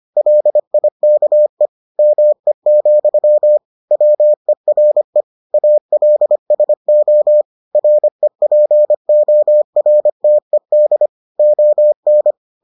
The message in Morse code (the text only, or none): LIKE ME, WERE ALSO REPORTED ON